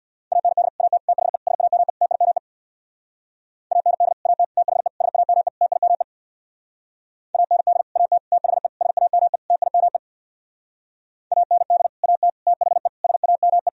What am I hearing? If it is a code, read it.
Morse code, heard as a keyed tone, about 64 wpm, 700 Hz. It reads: AND AT THE SAME TIME AND AT THE SAME TIME AND AT THE SAME TIME AND AT THE SAME